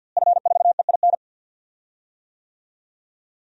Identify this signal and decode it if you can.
Morse code, heard as a keyed tone, about 47 wpm, 700 Hz. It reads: F4EIN
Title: DXCC-100 Call Signs - 5 Characters - Repeated 2x - 50wpm